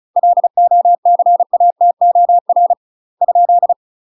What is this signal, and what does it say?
Morse code, heard as a keyed tone, about 36 wpm, 700 Hz. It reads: LOCATOR ?
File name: UK MASTER QSOs TRAINING / 35WPM / UK MASTER QSOs 0029